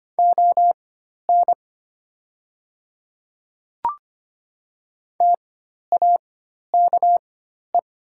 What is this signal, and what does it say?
Morse code, heard as a keyed tone, about 25 wpm, spaced as for 12 wpm, 700 Hz. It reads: ON TAKE
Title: 4x Character Spacing - Top 200 Words - 25wpm